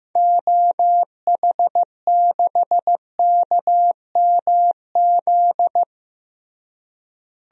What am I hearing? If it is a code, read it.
Morse code, heard as a keyed tone, about 15 wpm, 700 Hz. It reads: OH6KMZ
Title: DXCC-100 Call Signs - 6 Characters - Repeated 3x - 15wpm